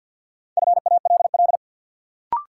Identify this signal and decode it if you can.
Morse code, heard as a keyed tone, about 50 wpm, 700 Hz. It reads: FALL
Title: Sentences from Top 500 Words - New Word Drill - 50wpm